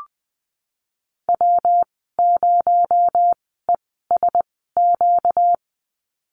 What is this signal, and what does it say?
Morse code, heard as a keyed tone, about 20 wpm, 700 Hz. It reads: W 0 E S Q
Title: 238 Lesson 38g - Letter q - Call Signs 20wpm